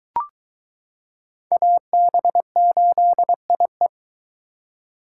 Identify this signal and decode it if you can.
Morse code, heard as a keyed tone, about 22 wpm, 700 Hz. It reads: AB8IE